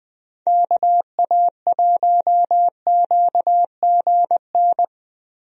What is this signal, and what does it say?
Morse code, heard as a keyed tone, about 21 wpm, 700 Hz. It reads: KA1QGN
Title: USA Call Signs - 6 Characters - Repeated 3x - 20wpm